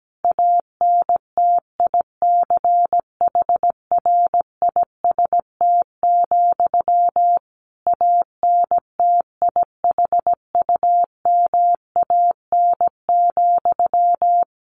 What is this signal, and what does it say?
Morse code, heard as a keyed tone, about 17 wpm, 700 Hz. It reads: ANTICHRIST, ANTIHUMAN,
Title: A Sound Of Thunder 17wpm